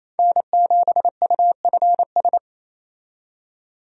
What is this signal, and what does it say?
Morse code, heard as a keyed tone, about 28 wpm, 700 Hz. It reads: N7UFS
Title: USA Call Signs - 5 Characters - 28wpm